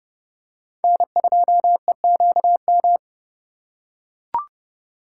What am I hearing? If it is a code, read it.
Morse code, heard as a keyed tone, about 27 wpm, 700 Hz. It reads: N2EQM